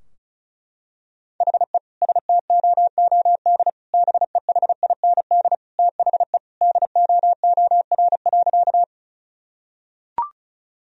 Morse code, heard as a keyed tone, about 36 wpm, 700 Hz. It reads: HE STOOD BEHIND THE DOOR.
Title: Sentences from Top 1000 Words - Review Part 2 - 35wpm